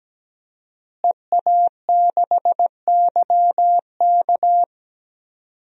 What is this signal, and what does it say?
Morse code, heard as a keyed tone, about 17 wpm, 700 Hz. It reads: EA6YK
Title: DXCC-100 Call Signs - 5 Characters - Repeated 3x - 17wpm